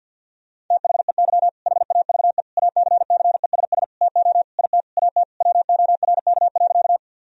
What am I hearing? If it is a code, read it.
Morse code, heard as a keyed tone, about 50 wpm, 700 Hz. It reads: THEY HAVE ACCESS TO IT AT WORK.
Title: Sentences from Top 1000 Words - Review Part 4 - 50wpm